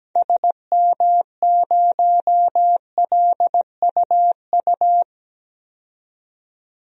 Morse code, heard as a keyed tone, about 17 wpm, 700 Hz. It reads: SM0LUU
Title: DXCC-100 Call Signs - 6 Characters - 17wpm